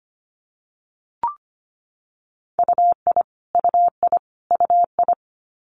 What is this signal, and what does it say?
Morse code, heard as a keyed tone, about 25 wpm, 700 Hz. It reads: UI UI UI
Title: Multi-letter Phonemes Repeated 3x 25wpm